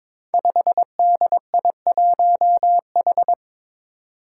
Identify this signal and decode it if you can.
Morse code, heard as a keyed tone, about 22 wpm, 700 Hz. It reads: 5DI1H
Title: Five Challenging Letter-Number Combinations 22wpm